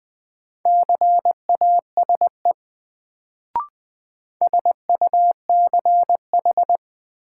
Morse code, heard as a keyed tone, about 20 wpm, 700 Hz. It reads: CASE SUCH
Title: Sets of 2 Words - New Word Drill - 20wpm